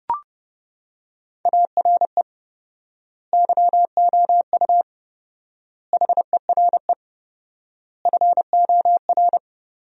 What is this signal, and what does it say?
Morse code, heard as a keyed tone, about 32 wpm, 700 Hz. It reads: ARE YOU HERE FOR
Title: 4x Word-Spacing - Sentences from Top 200 Words - 30wpm